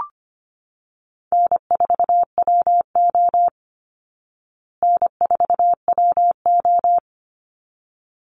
Morse code, heard as a keyed tone, about 25 wpm, 700 Hz. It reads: N4WO N4WO
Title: Call Signs Repeated 3x - 25wpm